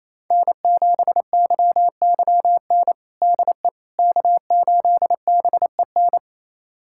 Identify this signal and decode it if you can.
Morse code, heard as a keyed tone, about 28 wpm, 700 Hz. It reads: N7YYN DE K8BEN
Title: USA Callsign DE Callsign - 5 Characters - 28wpm